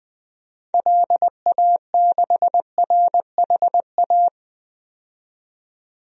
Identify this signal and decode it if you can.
Morse code, heard as a keyed tone, about 20 wpm, 700 Hz. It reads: LA6RHA
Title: DXCC-100 Call Signs - 6 Characters - 20wpm